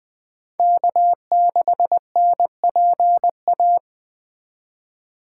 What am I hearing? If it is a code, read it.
Morse code, heard as a keyed tone, about 20 wpm, 700 Hz. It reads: K6NPA
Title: Speed Racing - USA 1x3 Call Signs 17wpm